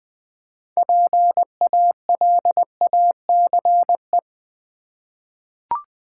Morse code, heard as a keyed tone, about 20 wpm, 700 Hz. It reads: PALACE
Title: 6 Letter Words - First 3 Letters Given - 20wpm